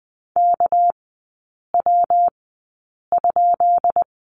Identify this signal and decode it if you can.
Morse code, heard as a keyed tone, about 20 wpm, spaced as for 8 wpm, 690 Hz. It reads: KW?
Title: 198 Lesson 32e - Letter k - Three Characters 20wpm